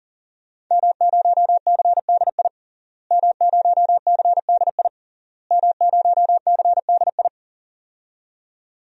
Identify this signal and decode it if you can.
Morse code, heard as a keyed tone, about 37 wpm, 700 Hz. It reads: M0CDI M0CDI M0CDI
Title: DXCC-100 Call Signs - 5 Characters - Repeated 3x - 40wpm